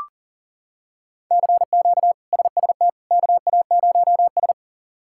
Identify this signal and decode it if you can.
Morse code, heard as a keyed tone, about 40 wpm, 700 Hz. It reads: CQ SST KA0S
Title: K1USN SST QSO Practice - 40wpm